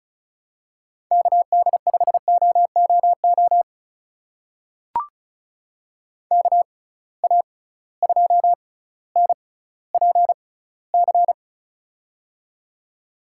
Morse code, heard as a keyed tone, about 36 wpm, spaced as for 18 wpm, 700 Hz. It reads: KD5OOO KA2NPC
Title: USA Call Signs - 6 Characters - 6x Character Spacing - 35wpm